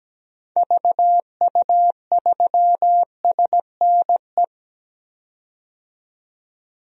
Morse code, heard as a keyed tone, about 17 wpm, 700 Hz. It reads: VU3SNE
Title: DXCC-100 Call Signs - 6 Characters - 17wpm